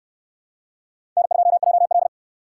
Morse code, heard as a keyed tone, about 68 wpm, 700 Hz. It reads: N2JL